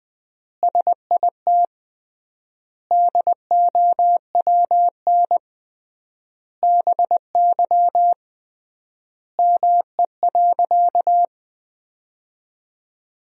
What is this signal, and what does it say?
Morse code, heard as a keyed tone, about 20 wpm, 700 Hz. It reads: SIT DOWN BY ME.
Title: 3x Word-Spacing - Sentences from Top 500 Words - 20wpm